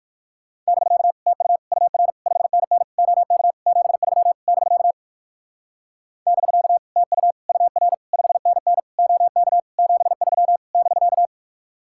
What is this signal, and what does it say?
Morse code, heard as a keyed tone, about 59 wpm, 700 Hz. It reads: <BK> TU UR 5NN OK 73 <BK> <BK> TU UR 5NN OK 73 <BK>